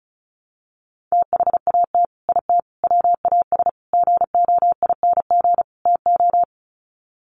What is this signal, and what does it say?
Morse code, heard as a keyed tone, about 34 wpm, 700 Hz. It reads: THAT IT WAS GOING TO
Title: Sets of 5 Words Encore2 35wpm